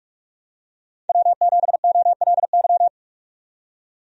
Morse code, heard as a keyed tone, about 46 wpm, 700 Hz. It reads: W7OLY
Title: USA Call Signs - 5 Characters - 45wpm